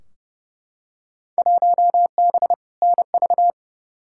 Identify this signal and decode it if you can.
Morse code, heard as a keyed tone, about 31 wpm, 700 Hz. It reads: 1B NV
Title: Field Day Exchanges - 30wpm